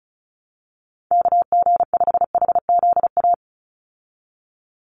Morse code, heard as a keyed tone, about 34 wpm, 700 Hz. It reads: KG5HZA
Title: USA Call Signs 35wpm